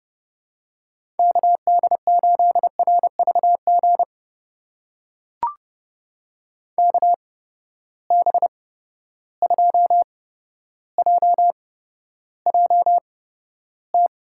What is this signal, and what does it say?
Morse code, heard as a keyed tone, about 32 wpm, spaced as for 13 wpm, 700 Hz. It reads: KD8RVG KB2JJT
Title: USA Call Signs - 6 Characters - 8x Character Spacing - 30wpm